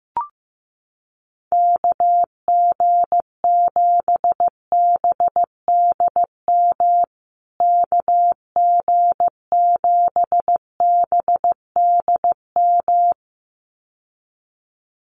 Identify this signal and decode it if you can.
Morse code, heard as a keyed tone, about 15 wpm, 700 Hz. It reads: KG7BDM KG7BDM
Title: USA Call Signs Repeated 2x 15wpm